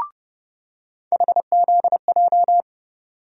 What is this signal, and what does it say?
Morse code, heard as a keyed tone, about 30 wpm, 700 Hz. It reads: HZJ
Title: USA Call Sign Postfixes - 3 Letters 30wpm